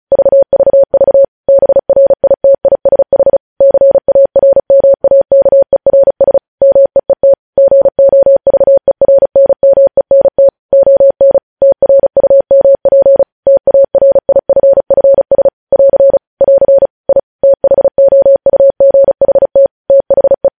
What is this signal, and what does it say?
Morse code, heard as a keyed tone, about 36 wpm, 550 Hz. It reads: VVV BRITISH CARMAKERS MEET GOVERNMENT ON TRUMP TARIFFS <AR> <AR> I THOUGHT THE
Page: News headlines in Morse code at 35 wpm